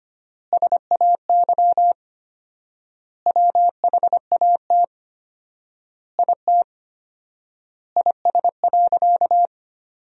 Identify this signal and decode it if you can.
Morse code, heard as a keyed tone, about 25 wpm, 700 Hz. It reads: SAY WHAT IT IS.
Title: 4x Word-Spacing - Sentences from Top 200 Words - 25wpm